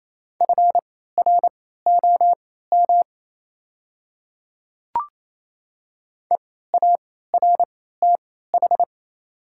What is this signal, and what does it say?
Morse code, heard as a keyed tone, about 28 wpm, spaced as for 16 wpm, 700 Hz. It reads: FROM EARTH
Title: 3x Character Spacing - Top 300 Words - 28wpm